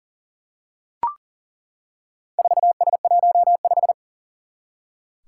Morse code, heard as a keyed tone, about 40 wpm, 700 Hz. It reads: 4S15
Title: Four Challenging Letter-Number Combinations 40wpm